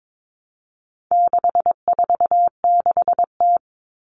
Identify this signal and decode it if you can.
Morse code, heard as a keyed tone, about 22 wpm, 700 Hz. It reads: 646T
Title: All Cut Numbers - 4-Digit 22wpm